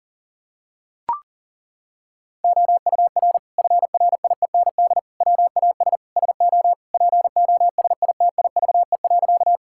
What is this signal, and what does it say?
Morse code, heard as a keyed tone, about 40 wpm, 700 Hz. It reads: OUR FRIEND WAS SO POSITIVE.
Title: Sentences from Top 1000 Words - Review Part 5 - 40wpm